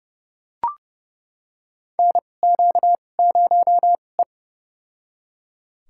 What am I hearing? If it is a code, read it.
Morse code, heard as a keyed tone, about 30 wpm, spaced as for 22 wpm, 700 Hz. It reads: NQ0E
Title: USA Call Signs - 4 Characters - 2x Character Spacing - 30wpm